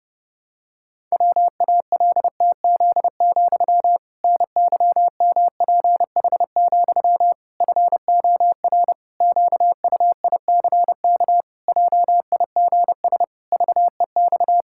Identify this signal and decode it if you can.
Morse code, heard as a keyed tone, about 30 wpm, 700 Hz. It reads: WALTZ, NYMPH, FOR QUICK JIGS VEX
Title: Pangram Challenge v2 - 30wpm